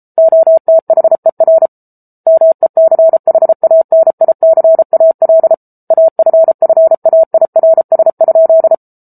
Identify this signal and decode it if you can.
Morse code, heard as a keyed tone, about 34 wpm, 650 Hz. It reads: OTHER MECHANICAL AFFAIRS?
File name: Book Two / 07-Chapter 7 - 33 WPM